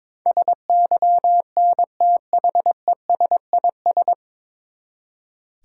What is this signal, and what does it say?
Morse code, heard as a keyed tone, about 22 wpm, 700 Hz. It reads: SYNTHESIS
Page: Morse Code Ninja practice 22wpm